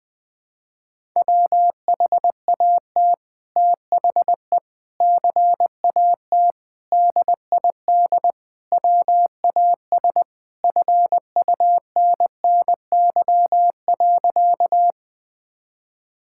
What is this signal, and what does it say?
Morse code, heard as a keyed tone, about 20 wpm, 700 Hz. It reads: WHAT THE CAT DID WAS FUNNY.
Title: Sentences with 2-Word Phrases - 20wpm